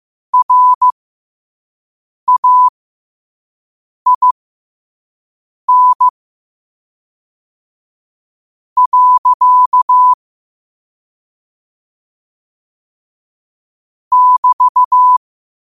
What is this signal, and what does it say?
Morse code, heard as a keyed tone, about 15 wpm, spaced as for 5 wpm, 1000 Hz. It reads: RAIN. <BT>
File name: qotdfiles / 5wpm